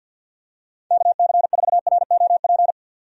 Morse code, heard as a keyed tone, about 50 wpm, 700 Hz. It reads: KC4ROP
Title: USA Call Signs - 6 Characters - 50wpm